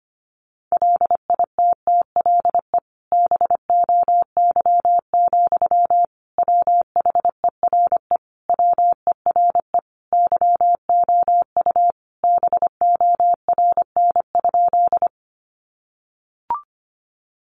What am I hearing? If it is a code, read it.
Morse code, heard as a keyed tone, about 25 wpm, 700 Hz. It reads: LITTLE BOY, WHERE WERE YOU BORN?
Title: Sentences from Top 1000 Words 25wpm